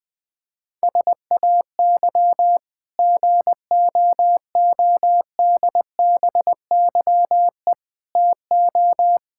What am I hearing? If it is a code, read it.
Morse code, heard as a keyed tone, about 20 wpm, 700 Hz. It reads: SAY GOODBYE TO
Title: Sentences with 5-Word Phrases - Spoken Beginning - 20wpm